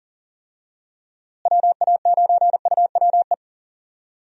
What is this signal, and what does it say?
Morse code, heard as a keyed tone, about 37 wpm, 700 Hz. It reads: WA9UWE
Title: USA Call Signs - 6 Characters - Repeated 3x - 40wpm